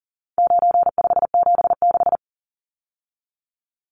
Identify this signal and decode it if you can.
Morse code, heard as a keyed tone, about 40 wpm, 700 Hz. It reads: 9576
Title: Numbers - 4-Digit 40wpm